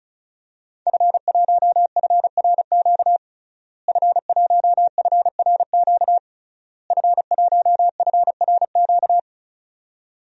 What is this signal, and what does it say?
Morse code, heard as a keyed tone, about 36 wpm, 700 Hz. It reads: F1FRQ F1FRQ F1FRQ
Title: DXCC-100 Call Signs - 5 Characters - Repeated 3x - 35wpm